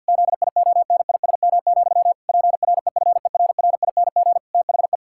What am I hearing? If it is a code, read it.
Morse code, heard as a keyed tone, about 50 wpm, 700 Hz. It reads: ZIONISM, PREFERRING THE
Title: Pangram Challenge v2 - 50wpm